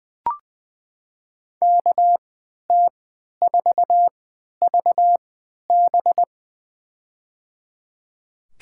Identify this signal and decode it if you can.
Morse code, heard as a keyed tone, about 21 wpm, spaced as for 12 wpm, 700 Hz. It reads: KT4VB
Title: USA Call Signs - 3x Character Spacing - 20wpm